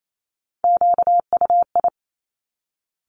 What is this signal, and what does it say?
Morse code, heard as a keyed tone, about 28 wpm, 700 Hz. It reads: QUI
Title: Top Three Letter Combinations in Words 28wpm